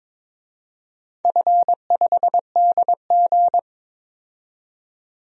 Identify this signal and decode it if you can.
Morse code, heard as a keyed tone, about 21 wpm, 700 Hz. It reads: F5DG
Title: DXCC-100 Call Signs - 4 Characters - 22wpm